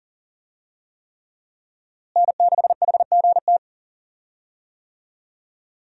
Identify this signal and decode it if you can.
Morse code, heard as a keyed tone, about 40 wpm, 700 Hz. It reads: N6HGT
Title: USA Call Signs - 5 Characters - Repeated 3x - 40wpm